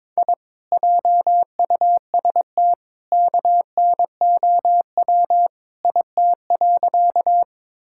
Morse code, heard as a keyed tone, about 22 wpm, 700 Hz. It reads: I JUST KNOW IT.
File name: practice / 22wpm / Sentences with 2-Word Phrases - 22wpm